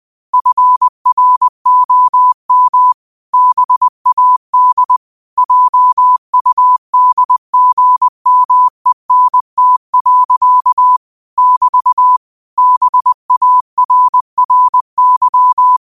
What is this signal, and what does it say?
Morse code, heard as a keyed tone, about 20 wpm, 1000 Hz. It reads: FROM BAD JUDGMENT. <BT> BARRY